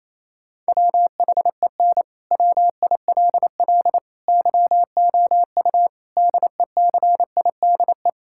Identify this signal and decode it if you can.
Morse code, heard as a keyed tone, about 28 wpm, 700 Hz. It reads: WHEN WILL YOU DECIDE
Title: Sentences from Top 1000 Words - Review Part 6 - 28wpm